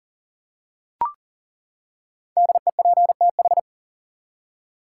Morse code, heard as a keyed tone, about 40 wpm, 700 Hz. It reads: DEPTH
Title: Five Letter Words - 40wpm